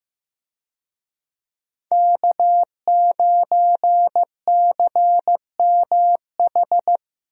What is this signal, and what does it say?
Morse code, heard as a keyed tone, about 15 wpm, 700 Hz. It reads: K9CMH
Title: USA Call Signs - 5 Characters - 15wpm